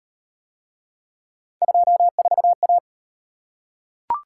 Morse code, heard as a keyed tone, about 38 wpm, 700 Hz. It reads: 24A